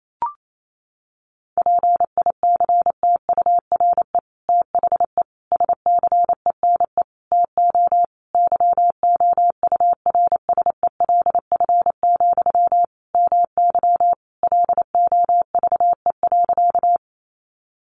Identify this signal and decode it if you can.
Morse code, heard as a keyed tone, about 28 wpm, 700 Hz. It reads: PICTURE THE SCENE TO YOURSELF, MY LOVE.
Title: Sentences from Top 1100 Words 28wpm